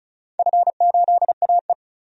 Morse code, heard as a keyed tone, about 36 wpm, 700 Hz. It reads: F8AE
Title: DXCC-100 Call Signs - 4 Characters - 35wpm